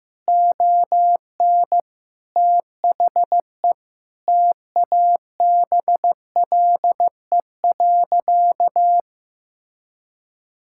Morse code, heard as a keyed tone, about 15 wpm, 700 Hz. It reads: ON THE TABLE.
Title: Sentences from Top 1000 Words - Review Part 6 - 15wpm